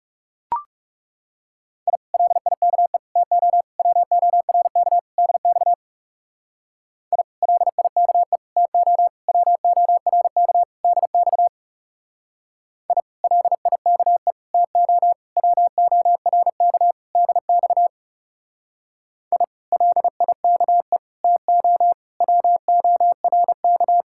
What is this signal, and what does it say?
Morse code, heard as a keyed tone, about 37 wpm, 700 Hz. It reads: I LIKE TO WORK DX I LIKE TO WORK DX I LIKE TO WORK DX I LIKE TO WORK
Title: Speed-Racing Common QSO Phrases - 30wpm